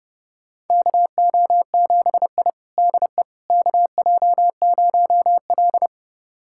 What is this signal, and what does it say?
Morse code, heard as a keyed tone, about 30 wpm, 700 Hz. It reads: KO7I DE KJ0L
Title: USA Callsign DE Callsign - 4 Characters - 30wpm